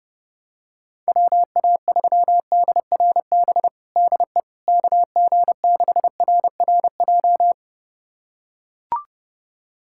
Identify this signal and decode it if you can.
Morse code, heard as a keyed tone, about 30 wpm, 700 Hz. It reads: WA3DRB DE KG6RRJ
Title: USA Callsign DE Callsign - 6 Characters - 30wpm